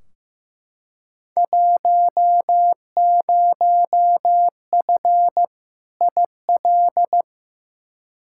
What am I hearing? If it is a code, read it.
Morse code, heard as a keyed tone, about 15 wpm, 700 Hz. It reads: 10F IL
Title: Field Day Exchanges Encore - 15wpm